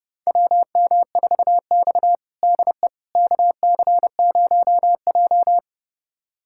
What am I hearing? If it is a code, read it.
Morse code, heard as a keyed tone, about 30 wpm, 700 Hz. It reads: WM4X DE KC0J
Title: USA Callsign DE Callsign - 4 Characters - 30wpm